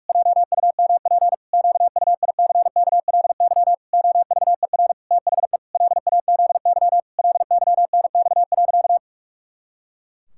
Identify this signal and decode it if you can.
Morse code, heard as a keyed tone, about 46 wpm, 700 Hz. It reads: JUMP QUICKLY OVER THE LAZY LYNX.
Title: Pangram Challenge v2 - 45wpm